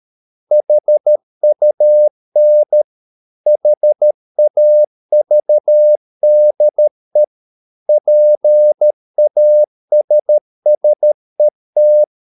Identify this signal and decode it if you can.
Morse code, heard as a keyed tone, about 13 wpm, 600 Hz. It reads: HUN HAVDE PASSET
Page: Ferie 13 wpm